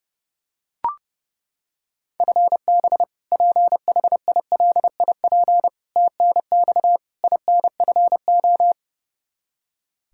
Morse code, heard as a keyed tone, about 32 wpm, 700 Hz. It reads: FB PHILIP TNX INFO
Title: Practice QSO - Exchange 3 - More Info - 30wpm